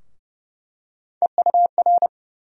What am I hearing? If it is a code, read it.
Morse code, heard as a keyed tone, about 31 wpm, 700 Hz. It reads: EUR
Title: USA Call Sign Postfixes - 3 Letters 30wpm